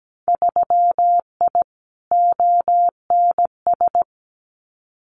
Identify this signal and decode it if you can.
Morse code, heard as a keyed tone, about 17 wpm, 700 Hz. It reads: 3I ONS
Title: Winter Field Day Exchanges Encore 17wpm